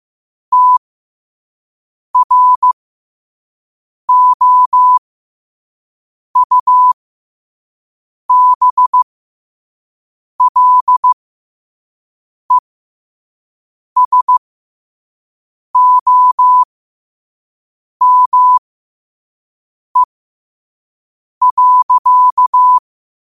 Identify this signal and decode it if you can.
Morse code, heard as a keyed tone, about 15 wpm, spaced as for 5 wpm, 1000 Hz. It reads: TROUBLESOME.
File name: qotdfiles / 5wpm